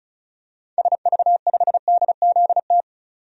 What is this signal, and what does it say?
Morse code, heard as a keyed tone, about 34 wpm, 700 Hz. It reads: SV5DZT
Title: DXCC-100 Call Signs - 6 Characters - 35wpm